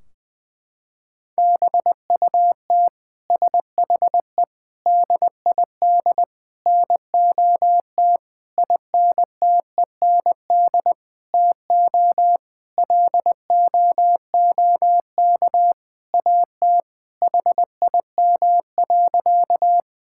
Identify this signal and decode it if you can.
Morse code, heard as a keyed tone, about 21 wpm, 700 Hz. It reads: BUT SHE DID NOT INTEND TO LOOK AT HIM.
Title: Sentences from Top 1600 Words 20wpm